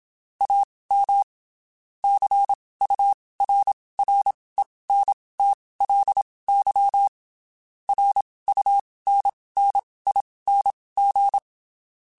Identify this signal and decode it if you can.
Morse code, heard as a keyed tone, about 27 wpm, spaced as for 19 wpm, 800 Hz. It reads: AM CURRENTLY RUNNING
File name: (spaced) / QSOs Spaced-mp3s / WD0FBT 1 @25 sp